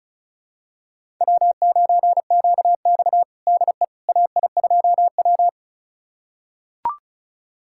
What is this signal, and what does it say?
Morse code, heard as a keyed tone, about 36 wpm, 700 Hz. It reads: W9QX DE AI2W
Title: USA Callsign DE Callsign - 4 Characters - 35wpm